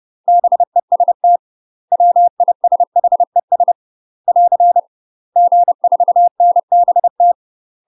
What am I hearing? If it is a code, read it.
Morse code, heard as a keyed tone, about 30 wpm, 700 Hz. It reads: BEST WISHES <AR> G4NBT
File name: UK MASTER QSOs TRAINING / 30WPM / UK MASTER QSOs 0040